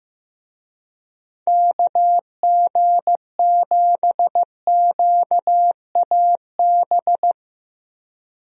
Morse code, heard as a keyed tone, about 15 wpm, 700 Hz. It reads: KG7QAB
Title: USA Call Signs - 6 Characters - Repeated 3x - 15wpm